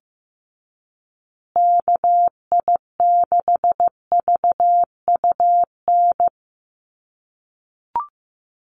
Morse code, heard as a keyed tone, about 15 wpm, 700 Hz. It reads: KI6VUN
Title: USA Call Signs Repeated 2x 15wpm